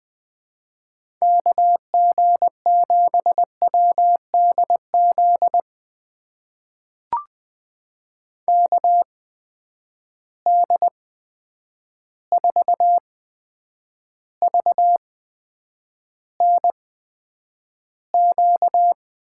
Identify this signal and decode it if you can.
Morse code, heard as a keyed tone, about 20 wpm, spaced as for 9 wpm, 700 Hz. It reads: KG7WDZ KD4VNQ
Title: USA Call Signs - 6 Characters - 8x Character Spacing - 20wpm